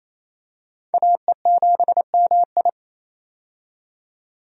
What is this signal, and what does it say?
Morse code, heard as a keyed tone, about 28 wpm, 700 Hz. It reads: AE7MI
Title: USA Call Signs - 5 Characters - 28wpm